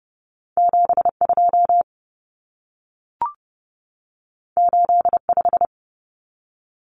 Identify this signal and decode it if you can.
Morse code, heard as a keyed tone, about 30 wpm, 700 Hz. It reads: 72 85
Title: Numbers - 2-Digit 30wpm